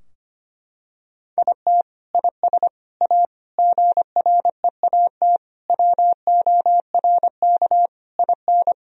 Morse code, heard as a keyed tone, about 25 wpm, 700 Hz. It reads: IT IS A GREAT WORK IN
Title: Sentences from Top 1000 Words - Review Part 3 - 25wpm